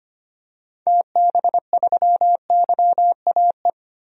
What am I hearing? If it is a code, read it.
Morse code, heard as a keyed tone, about 25 wpm, 700 Hz. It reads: TB3YAE
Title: DXCC-100 Call Signs - 6 Characters - 25wpm